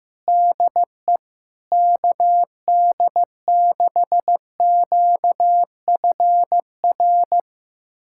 Morse code, heard as a keyed tone, about 16 wpm, 700 Hz. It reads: DE KD6QFR
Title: USA Callsign DE Callsign 15wpm